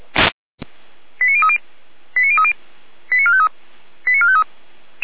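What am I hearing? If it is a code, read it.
Five-tone selective call (ZVEI), heard as quick runs of five tones, tones of about 70 ms. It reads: 89030 89030 89453 89453
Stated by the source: ZVEI five-tone selective call received on SDR.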